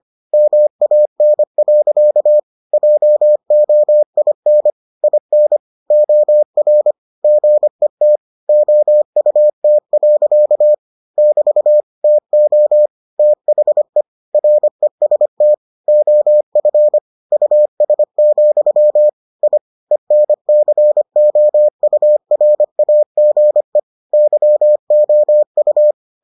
Morse code, heard as a keyed tone, about 25 wpm, 600 Hz. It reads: MAN. JOIN IN OR GET OUT. <BT> TO THE REST OF US, I ENCOURAGE YOU